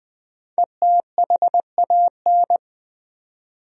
Morse code, heard as a keyed tone, about 20 wpm, 700 Hz. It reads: ETHAN
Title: USA Common Names - 20wpm